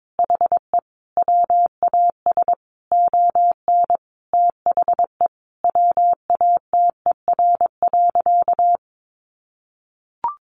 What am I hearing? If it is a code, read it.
Morse code, heard as a keyed tone, about 22 wpm, 700 Hz. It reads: HE WAS ON THE WATER.
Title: Sentences from Top 500 Words 22wpm